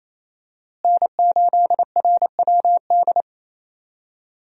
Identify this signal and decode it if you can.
Morse code, heard as a keyed tone, about 28 wpm, 700 Hz. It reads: N8RWD
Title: USA Call Signs - 5 Characters - Repeated 2x - 28wpm